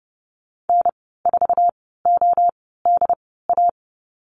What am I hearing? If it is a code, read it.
Morse code, heard as a keyed tone, about 30 wpm, spaced as for 17 wpm, 700 Hz. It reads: N4ODA
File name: Lesson 15g - Number 5 - Call Signs 30wpm